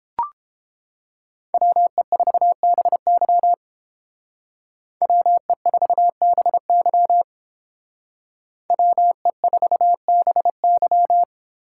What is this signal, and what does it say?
Morse code, heard as a keyed tone, about 29 wpm, 700 Hz. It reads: WE4BY WE4BY WE4BY